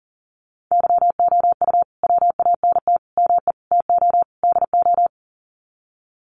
Morse code, heard as a keyed tone, about 40 wpm, 700 Hz. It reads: YOU WANT ME TO DO
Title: Sets of 5 Words Encore2 40wpm